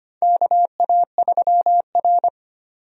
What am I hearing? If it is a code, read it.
Morse code, heard as a keyed tone, about 25 wpm, 700 Hz. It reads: KA3R
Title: USA Call Signs - 4 Characters - 25wpm